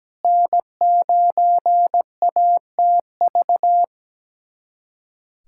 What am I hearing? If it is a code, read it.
Morse code, heard as a keyed tone, about 17 wpm, 700 Hz. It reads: N9ATV
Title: USA Callsign DE Callsign - 5 Characters - 17wpm